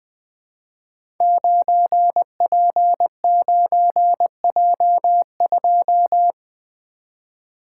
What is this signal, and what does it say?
Morse code, heard as a keyed tone, about 20 wpm, 700 Hz. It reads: 9P9J2
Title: Code Groups - 5 Letter:Numbers - 20wpm